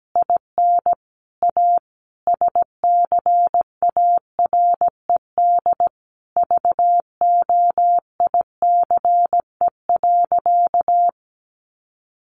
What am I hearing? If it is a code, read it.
Morse code, heard as a keyed tone, about 17 wpm, 700 Hz. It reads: IN A SCARED VOICE.
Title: Sentences from Top 700 Words 17wpm